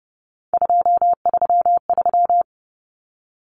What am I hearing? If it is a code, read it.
Morse code, heard as a keyed tone, about 30 wpm, 700 Hz. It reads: 233
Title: Common Cut Numbers - 3-Digit 30wpm